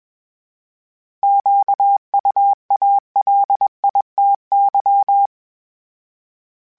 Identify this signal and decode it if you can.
Morse code, heard as a keyed tone, about 21 wpm, 800 Hz. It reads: QUALITY
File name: Misc / Misc-mp3s / Misc @20